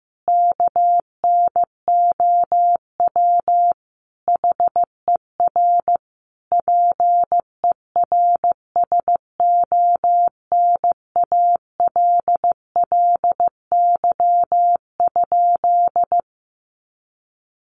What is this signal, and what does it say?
Morse code, heard as a keyed tone, about 15 wpm, 700 Hz. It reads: KNOW HER PERSONALLY?
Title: Sentences from Top 1400 Words 15wpm